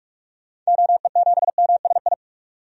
Morse code, heard as a keyed tone, about 44 wpm, 700 Hz. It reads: OE7MSI
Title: DXCC-100 Call Signs - 6 Characters - 45wpm